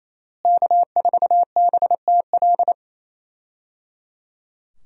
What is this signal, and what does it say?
Morse code, heard as a keyed tone, about 27 wpm, 700 Hz. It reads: K4BTL
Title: USA Call Signs - 5 Characters - Repeated 3x - 28wpm